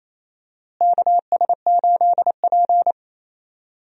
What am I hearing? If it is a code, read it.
Morse code, heard as a keyed tone, about 28 wpm, 700 Hz. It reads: KS8P
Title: USA Call Signs - 4 Characters - Repeated 3x - 28wpm